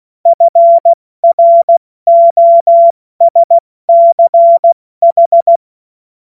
Morse code, heard as a keyed tone, about 16 wpm, 680 Hz. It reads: FROSCH
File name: Morsekurs / Witze / Witz 08. November 2022 70 F80